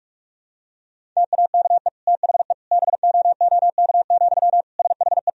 Morse code, heard as a keyed tone, about 46 wpm, 700 Hz. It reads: TAKE THE BOOK, SHE
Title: Sentences from Top 1000 Words - Review Part 1 - 45wpm